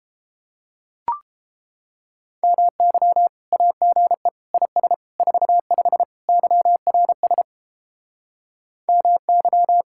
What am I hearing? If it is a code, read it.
Morse code, heard as a keyed tone, about 31 wpm, 700 Hz. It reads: MY AGE IS 45 YRS MY